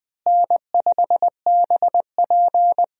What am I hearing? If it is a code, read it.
Morse code, heard as a keyed tone, about 20 wpm, 700 Hz. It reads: N5BP
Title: USA Call Signs - 4 Characters - Repeated 3x - 20wpm